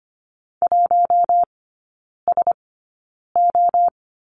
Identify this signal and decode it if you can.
Morse code, heard as a keyed tone, about 25 wpm, spaced as for 9 wpm, 690 Hz. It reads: 1SO
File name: Lesson 14e - Number 2 - Three Characters 25wpm